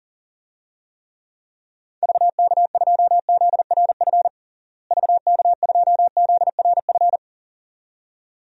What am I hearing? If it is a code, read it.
Morse code, heard as a keyed tone, about 37 wpm, 700 Hz. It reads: VK2ZRF VK2ZRF
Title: DXCC-100 Call Signs - 6 Characters - Repeated 2x - 40wpm